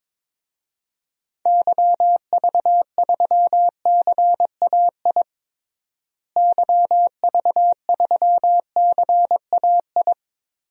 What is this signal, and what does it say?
Morse code, heard as a keyed tone, about 22 wpm, 700 Hz. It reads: YV3CAI YV3CAI
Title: DXCC-100 Call Signs - 6 Characters - Repeated 3x - 22wpm